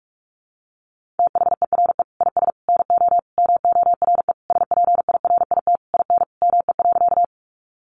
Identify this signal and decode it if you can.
Morse code, heard as a keyed tone, about 44 wpm, 700 Hz. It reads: THERE IS NO MORE SPIRIT IN ME.
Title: Sentences from Top 1400 Words 45wpm